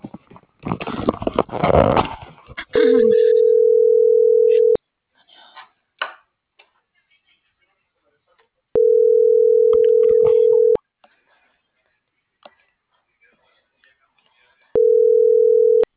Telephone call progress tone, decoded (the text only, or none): ringback tone